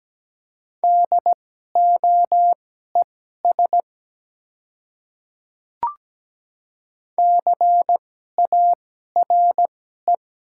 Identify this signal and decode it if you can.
Morse code, heard as a keyed tone, about 17 wpm, spaced as for 12 wpm, 700 Hz. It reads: DOES CARE
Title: 2x Character Spacing - Top 300 Words - 17wpm